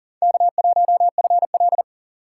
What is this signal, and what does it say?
Morse code, heard as a keyed tone, about 40 wpm, 700 Hz. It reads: K1FL
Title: USA Call Signs - 4 Characters - 2x Character Spacing - 40wpm